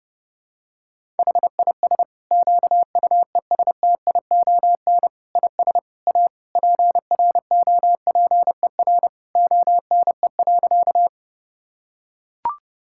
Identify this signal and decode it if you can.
Morse code, heard as a keyed tone, about 31 wpm, 700 Hz. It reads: HIS QUESTION IS A PROPER ONE.
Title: Sentences from Top 1000 Words - Review Part 5 - 30wpm